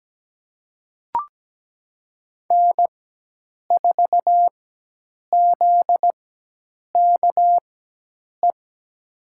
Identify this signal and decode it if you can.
Morse code, heard as a keyed tone, about 17 wpm, spaced as for 8 wpm, 700 Hz. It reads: N4ZKE
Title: USA Call Signs - 5 Characters - 4x Character Spacing - 17wpm